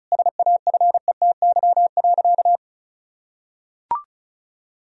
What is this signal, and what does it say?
Morse code, heard as a keyed tone, about 36 wpm, 700 Hz. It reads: SAFETY.
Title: Sentences from Top 1000 Words - Review Part 4 - 35wpm